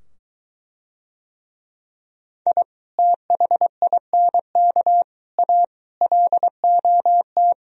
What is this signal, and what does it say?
Morse code, heard as a keyed tone, about 23 wpm, 700 Hz. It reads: I THINK A LOT